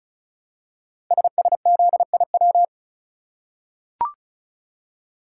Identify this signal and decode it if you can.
Morse code, heard as a keyed tone, about 36 wpm, 700 Hz. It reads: SSZIW
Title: Code Groups - 5 Letters - 35wpm